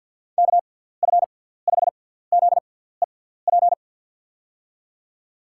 Morse code, heard as a keyed tone, about 52 wpm, spaced as for 17 wpm, 700 Hz. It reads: KF5ZEP
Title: USA Call Signs - 6 Characters - 6x Character Spacing - 50wpm